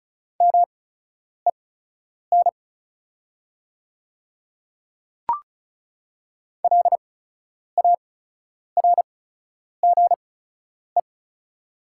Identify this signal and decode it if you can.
Morse code, heard as a keyed tone, about 34 wpm, spaced as for 10 wpm, 700 Hz. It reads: MEN LARGE
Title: 8x Character Spacing - Top 200 Words - 35wpm